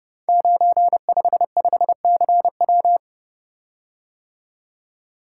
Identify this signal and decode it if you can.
Morse code, heard as a keyed tone, about 30 wpm, 700 Hz. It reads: 955CW
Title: Code Groups - 5 Letter:Numbers - 30wpm